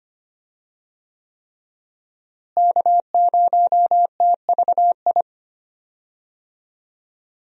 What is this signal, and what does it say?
Morse code, heard as a keyed tone, about 25 wpm, 700 Hz. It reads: K0TVI
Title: USA Call Signs - 5 Characters - Repeated 3x - 25wpm